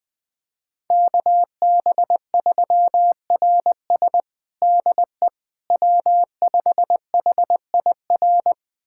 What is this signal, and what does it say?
Morse code, heard as a keyed tone, about 20 wpm, 700 Hz. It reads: KB3RS DE W5HIR
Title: USA Callsign DE Callsign 20wpm